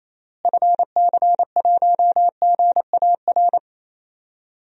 Morse code, heard as a keyed tone, about 28 wpm, 700 Hz. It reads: FC1GAR
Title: DXCC-100 Call Signs - 6 Characters - Repeated 3x - 28wpm